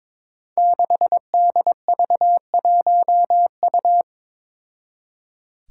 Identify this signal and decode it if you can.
Morse code, heard as a keyed tone, about 22 wpm, 700 Hz. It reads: 6DV1U
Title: Five Challenging Letter-Number Combinations 22wpm